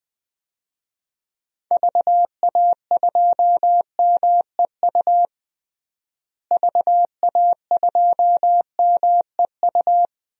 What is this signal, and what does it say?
Morse code, heard as a keyed tone, about 20 wpm, 700 Hz. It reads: VA2MEU VA2MEU
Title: DXCC-100 Call Signs - 6 Characters - Repeated 3x - 20wpm